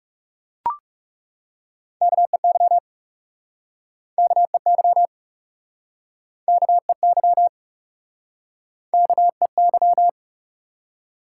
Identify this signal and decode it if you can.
Morse code, heard as a keyed tone, about 40 wpm, 700 Hz. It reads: KEY KEY KEY KEY